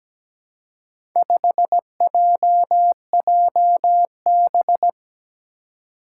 Morse code, heard as a keyed tone, about 17 wpm, 700 Hz. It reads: 5JJB
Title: Four Challenging Letter-Number Combinations 17wpm